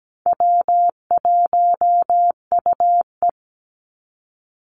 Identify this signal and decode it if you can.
Morse code, heard as a keyed tone, about 17 wpm, 700 Hz. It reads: W1UE
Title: Easy Call Signs Repeated 3x - 17wpm